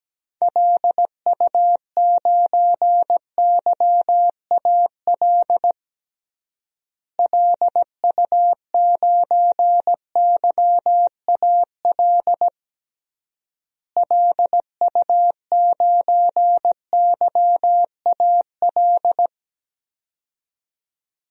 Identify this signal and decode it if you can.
Morse code, heard as a keyed tone, about 17 wpm, 700 Hz. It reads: LU9YAL LU9YAL LU9YAL
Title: DXCC-100 Call Signs - 6 Characters - Repeated 3x - 17wpm